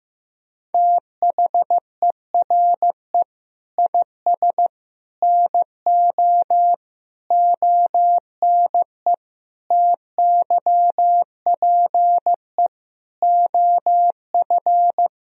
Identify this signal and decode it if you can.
Morse code, heard as a keyed tone, about 15 wpm, 700 Hz. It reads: THERE IS NO ONE TYPE OF
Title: Sentences from Top 1000 Words - Review Part 3 - 15wpm